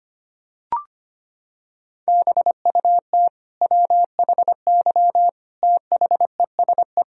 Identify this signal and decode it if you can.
Morse code, heard as a keyed tone, about 25 wpm, 700 Hz. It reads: BUT WHY THESE
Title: Sentences from Top 1000 Words - Review Part 1 - 25wpm